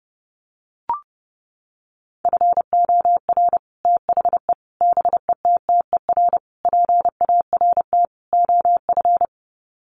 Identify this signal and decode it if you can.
Morse code, heard as a keyed tone, about 30 wpm, 700 Hz. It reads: FOR THE BETTER PART OF
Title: Sets of 5 Words Encore2 30wpm